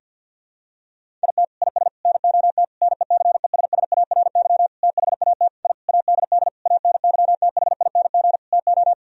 Morse code, heard as a keyed tone, about 52 wpm, 700 Hz. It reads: IT IS NOT NECESSARY THAT I ADD ANYTHING TO
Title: Sentences from Top 1000 Words - Review Part 1 - 50wpm